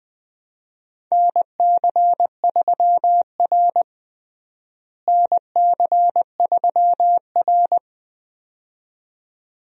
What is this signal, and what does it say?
Morse code, heard as a keyed tone, about 20 wpm, 700 Hz. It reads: NC3R NC3R
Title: USA Call Signs - 4 Characters - Repeated 2x - 20wpm